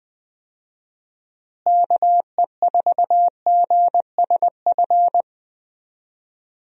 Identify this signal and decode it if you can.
Morse code, heard as a keyed tone, about 21 wpm, 700 Hz. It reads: KE4GSF
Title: USA Call Signs - 6 Characters - Repeated 2x - 20wpm